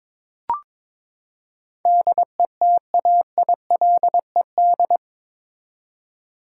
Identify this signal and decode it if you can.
Morse code, heard as a keyed tone, about 22 wpm, 700 Hz. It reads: DETAILED
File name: practice / 22wpm / Eight Letter Words - 22wpm